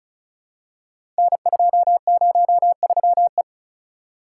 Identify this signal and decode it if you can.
Morse code, heard as a keyed tone, about 36 wpm, 700 Hz. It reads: N203E
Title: Code Groups - 5 Letter:Numbers - 35wpm